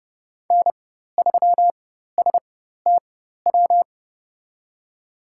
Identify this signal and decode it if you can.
Morse code, heard as a keyed tone, about 30 wpm, spaced as for 14 wpm, 700 Hz. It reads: N3STW
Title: USA Call Signs - 5 Characters - 4x Character Spacing - 30wpm